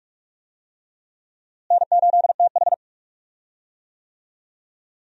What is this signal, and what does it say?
Morse code, heard as a keyed tone, about 45 wpm, 700 Hz. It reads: N8TH
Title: USA Call Signs - 4 Characters - Repeated 3x - 45wpm